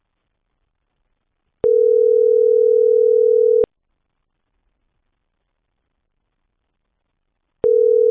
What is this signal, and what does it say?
Ringback tone (North America), 440 + 480 Hz.